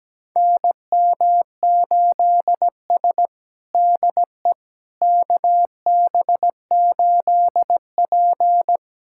Morse code, heard as a keyed tone, about 17 wpm, 700 Hz. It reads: NM8S DE KB8P
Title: USA Callsign DE Callsign - 4 Characters - 17wpm